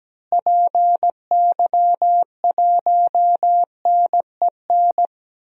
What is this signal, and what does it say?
Morse code, heard as a keyed tone, about 17 wpm, 700 Hz. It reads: PY1NEN
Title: DXCC-100 Call Signs - 6 Characters - Repeated 3x - 17wpm